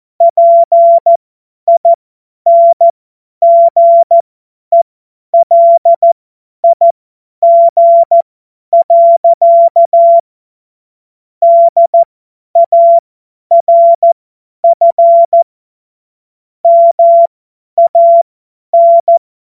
Morse code, heard as a keyed tone, about 14 wpm, spaced as for 10 wpm, 680 Hz. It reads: PINGELIG. DARF MAN
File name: Morsekurs / Witze / Witz 07.November 2023 50 F70